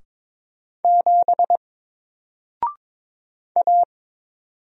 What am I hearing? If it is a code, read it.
Morse code, heard as a keyed tone, about 22 wpm, 700 Hz. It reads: 7 A
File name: Single-Letter-Number-22wpm